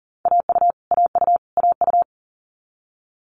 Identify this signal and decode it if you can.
Morse code, heard as a keyed tone, about 40 wpm, 700 Hz. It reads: AU AU AU
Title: Multi-letter Phonemes Repeated 3x 40wpm